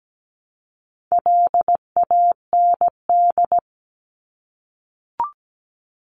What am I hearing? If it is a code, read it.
Morse code, heard as a keyed tone, about 17 wpm, 700 Hz. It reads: LAND
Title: Top 500 Words 17wpm